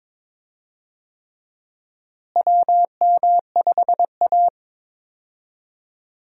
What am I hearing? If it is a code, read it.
Morse code, heard as a keyed tone, about 22 wpm, 700 Hz. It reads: WM5A
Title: USA Call Signs - 4 Characters - Repeated 3x - 22wpm